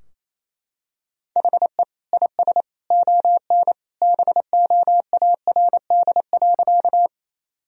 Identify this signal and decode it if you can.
Morse code, heard as a keyed tone, about 27 wpm, 700 Hz. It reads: HE IS ON BOARD.
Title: Sentences from Top 1000 Words - Review Part 4 - 28wpm